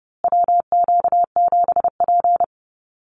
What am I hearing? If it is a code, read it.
Morse code, heard as a keyed tone, about 30 wpm, 700 Hz. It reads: WQ7P
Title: USA Call Signs Repeated 2x 30wpm